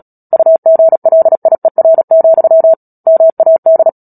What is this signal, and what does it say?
Morse code, heard as a keyed tone, about 37 wpm, 650 Hz. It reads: UGLIER, MAD
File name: Book Two / 10-Chapter 10 - Epilogue - 36 WPM